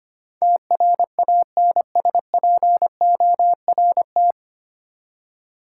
Morse code, heard as a keyed tone, about 25 wpm, 700 Hz. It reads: TRANSPORT
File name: Nine Letter Words - 25wpm